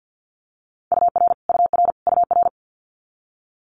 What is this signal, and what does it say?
Morse code, heard as a keyed tone, about 50 wpm, 700 Hz. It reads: UR UR UR
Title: Multi-letter Phonemes Repeated 3x 50wpm